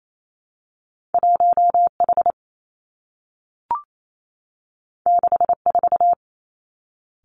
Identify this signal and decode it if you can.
Morse code, heard as a keyed tone, about 28 wpm, 700 Hz. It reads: 1 H 6 4
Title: Two Challenging Letter-Number Combinations 28wpm